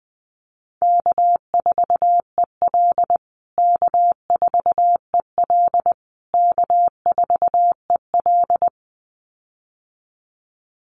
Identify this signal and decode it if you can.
Morse code, heard as a keyed tone, about 20 wpm, 700 Hz. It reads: K4EL K4EL K4EL
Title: USA Call Signs Repeated 3x 20wpm